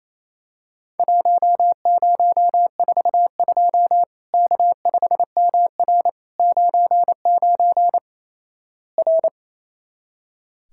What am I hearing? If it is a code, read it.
Morse code, heard as a keyed tone, about 27 wpm, 700 Hz. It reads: 1042 K5MR 99 R
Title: Worked All Europe DX Contest - QTC Exchange - 28wpm